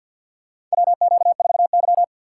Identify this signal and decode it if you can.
Morse code, heard as a keyed tone, about 52 wpm, 700 Hz. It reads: WQ4Y
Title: USA Call Signs - 4 Characters - 2x Character Spacing - 50wpm